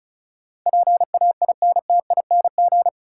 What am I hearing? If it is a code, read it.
Morse code, heard as a keyed tone, about 36 wpm, 700 Hz. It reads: PAINTING
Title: Sentences from Top 1000 Words - Review Part 6 - New Word Drill - 35wpm